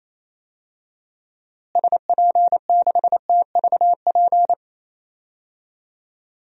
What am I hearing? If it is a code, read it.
Morse code, heard as a keyed tone, about 28 wpm, 700 Hz. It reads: SP6TVP
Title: DXCC-100 Call Signs - 6 Characters - Repeated 2x - 28wpm